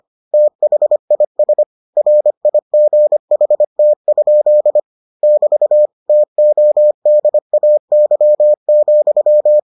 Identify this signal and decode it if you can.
Morse code, heard as a keyed tone, about 25 wpm, 600 Hz. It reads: THIS RIGHT? <BT> TODAY,